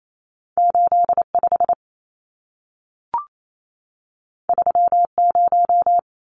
Morse code, heard as a keyed tone, about 28 wpm, 700 Hz. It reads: 85 30
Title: Numbers - 2-Digit 28wpm